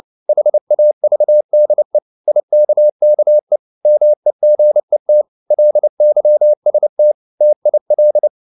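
Morse code, heard as a keyed tone, about 29 wpm, 600 Hz. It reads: HAVDE IKKE MEGET LYST TIL
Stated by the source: Fast (QRQ) Morse at 29 wpm, from CW med Gnister.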